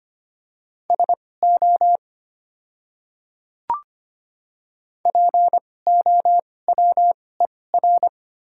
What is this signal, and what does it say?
Morse code, heard as a keyed tone, about 25 wpm, spaced as for 18 wpm, 700 Hz. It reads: SO POWER
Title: 2x Character Spacing - Top 500 Words - 25wpm